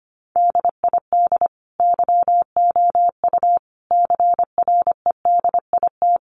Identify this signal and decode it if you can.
Morse code, heard as a keyed tone, about 25 wpm, 700 Hz. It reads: DID YOU CREDIT
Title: Sentences from Top 1000 Words 25wpm